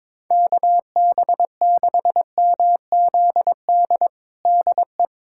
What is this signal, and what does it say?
Morse code, heard as a keyed tone, about 22 wpm, 700 Hz. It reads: KB6MZD DE
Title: USA Callsign DE Callsign - 6 Characters - 22wpm